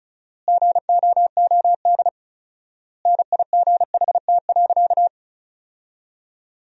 Morse code, heard as a keyed tone, about 36 wpm, 700 Hz. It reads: GOOD NIGHT.
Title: 4x Word-Spacing - Sentences from Top 300 Words - 35wpm